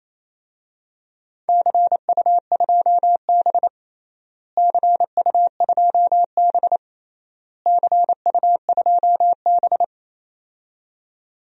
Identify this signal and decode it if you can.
Morse code, heard as a keyed tone, about 28 wpm, 700 Hz. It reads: CU2B CU2B CU2B
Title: DXCC-100 Call Signs - 4 Characters - Repeated 3x - 28wpm